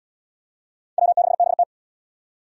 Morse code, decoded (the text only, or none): W6BT